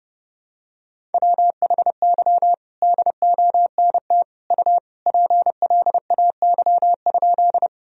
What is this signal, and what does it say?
Morse code, heard as a keyed tone, about 32 wpm, 700 Hz. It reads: WHY DONT U PLAY?
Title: Sentences from Top 300 Words and Abbreviations 30wpm